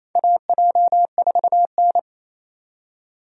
Morse code, heard as a keyed tone, about 28 wpm, 700 Hz. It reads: AJ4N
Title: USA Call Signs - 4 Characters - 28wpm